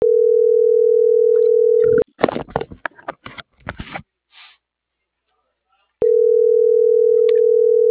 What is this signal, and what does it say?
Ringback tone (North America), 440 + 480 Hz.